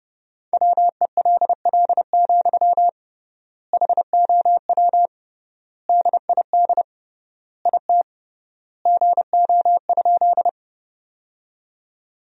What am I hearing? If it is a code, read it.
Morse code, heard as a keyed tone, about 32 wpm, 700 Hz. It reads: WELL, HOW DID IT GO?
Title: 3x Word-Spacing - Sentences from Top 200 Words - 30wpm